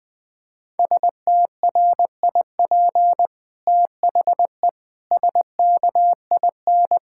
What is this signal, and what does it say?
Morse code, heard as a keyed tone, about 20 wpm, 700 Hz. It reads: STRIP THE SKIN
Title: Sentences from Top 1600 Words 20wpm